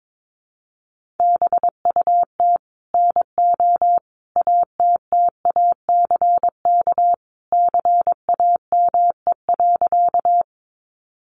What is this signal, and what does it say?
Morse code, heard as a keyed tone, about 22 wpm, 700 Hz. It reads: BUT NO ATTACK CAME.
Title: Sentences from Top 1500 Words 22wpm